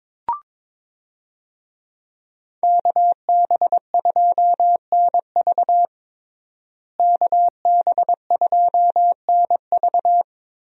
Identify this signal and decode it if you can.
Morse code, heard as a keyed tone, about 21 wpm, 700 Hz. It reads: KB2NV KB2NV
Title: USA Call Signs - 5 Characters - Repeated 2x - 22wpm